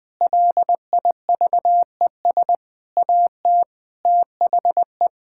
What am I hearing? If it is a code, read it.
Morse code, heard as a keyed tone, about 20 wpm, 700 Hz. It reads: LIVES AT THE
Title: Sentences with 5-Word Phrases - Spoken Beginning - 20wpm